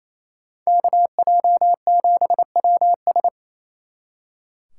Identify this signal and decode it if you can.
Morse code, heard as a keyed tone, about 30 wpm, 700 Hz. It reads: KJ7WS
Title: Speed Racing - USA Call Signs 28wpm